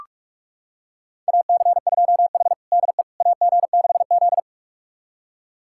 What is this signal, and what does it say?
Morse code, heard as a keyed tone, about 46 wpm, 700 Hz. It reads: AC2H DE AG6Z
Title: USA Callsign DE Callsign - 4 Characters - 45wpm